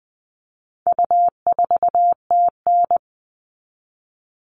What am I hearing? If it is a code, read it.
Morse code, heard as a keyed tone, about 20 wpm, 700 Hz. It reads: U4TN
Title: All Cut Numbers - 4-Digit 20wpm